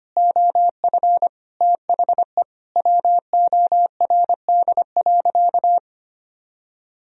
Morse code, heard as a keyed tone, about 25 wpm, 700 Hz. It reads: OF THE WORD.
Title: Sentences from Top 1000 Words - Review Part 4 - 25wpm